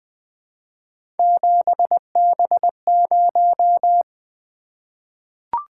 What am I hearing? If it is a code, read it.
Morse code, heard as a keyed tone, about 20 wpm, 700 Hz. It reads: 7B0
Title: Three Letter-Number Combinations - 20wpm